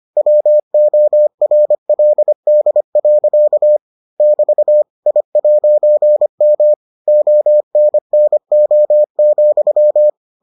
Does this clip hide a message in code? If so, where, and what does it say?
Yes: Morse code from 0.2 to 10.1 s, heard as a keyed tone, about 25 wpm, 600 Hz. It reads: WORLD. <BT> I'M ONNO,